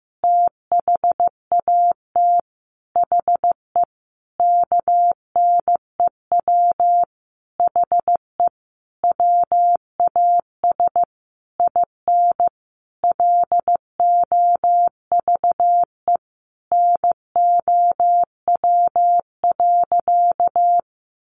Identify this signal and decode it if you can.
Morse code, heard as a keyed tone, about 15 wpm, 700 Hz. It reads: THAT HE KNEW HE WAS IN LOVE NOW.
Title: Sentences from Top 1300 Words 15wpm